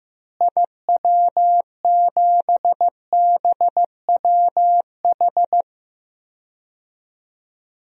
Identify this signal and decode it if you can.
Morse code, heard as a keyed tone, about 15 wpm, 700 Hz. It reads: IW7BWH
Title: DXCC-100 Call Signs - 6 Characters - Repeated 2x - 15wpm